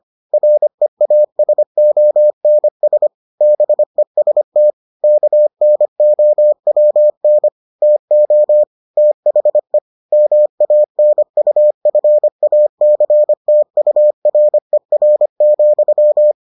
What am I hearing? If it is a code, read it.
Morse code, heard as a keyed tone, about 25 wpm, 600 Hz. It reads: REASONS BEST KNOWN TO THE MANUFACTURER,